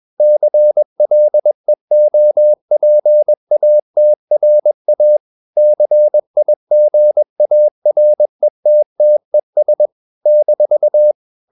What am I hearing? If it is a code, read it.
Morse code, heard as a keyed tone, about 21 wpm, 600 Hz. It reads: CLEOPATRA CIGARETTES -
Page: CW med Gnister